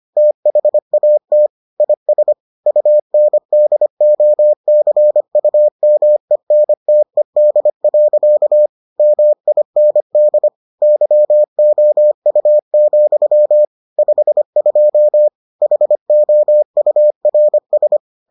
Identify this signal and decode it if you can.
Morse code, heard as a keyed tone, about 25 wpm, 600 Hz. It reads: THAT IS UNDOCUMENTED. MIND YOU, 52 HOURS